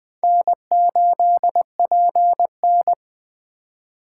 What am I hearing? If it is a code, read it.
Morse code, heard as a keyed tone, about 21 wpm, 700 Hz. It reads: N8PN
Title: Speed Racing - USA Call Signs 17wpm